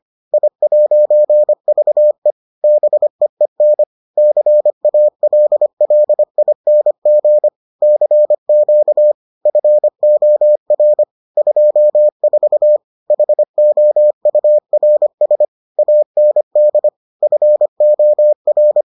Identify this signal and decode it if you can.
Morse code, heard as a keyed tone, about 25 wpm, 600 Hz. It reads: I'VE BEEN CALLING CQ FOR 24 HOURS AND FOR